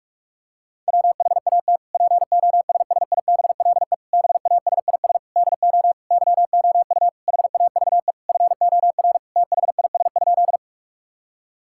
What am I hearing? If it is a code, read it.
Morse code, heard as a keyed tone, about 46 wpm, 700 Hz. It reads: WHAT POSSIBLE BASIS DO YOU HAVE FOR THIS?
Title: Sentences from Top 1000 Words - Review Part 5 - 45wpm